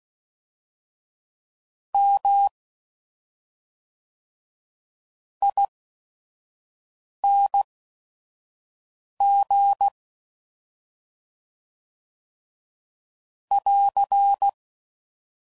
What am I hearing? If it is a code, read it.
Morse code, heard as a keyed tone, about 16 wpm, spaced as for 5 wpm, 800 Hz. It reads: M ING <AR>
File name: Suffixes / Special ING's / SlowToFaster ING's